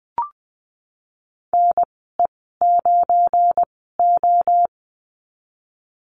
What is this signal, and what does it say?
Morse code, heard as a keyed tone, about 20 wpm, spaced as for 15 wpm, 700 Hz. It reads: NE9O
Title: 193 Lesson 31g - Letter v - Call Signs 20wpm